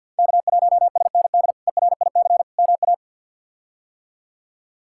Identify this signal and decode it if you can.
Morse code, heard as a keyed tone, about 53 wpm, 700 Hz. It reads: K1SND ERIC MA
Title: K1USN SST QSO Practice - 50wpm